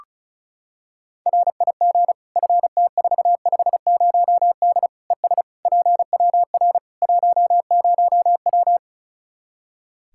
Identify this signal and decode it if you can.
Morse code, heard as a keyed tone, about 36 wpm, 700 Hz. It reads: RIG FT450D ES PWR 10W
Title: Practice QSO - Exchange 2 - Info - 35wpm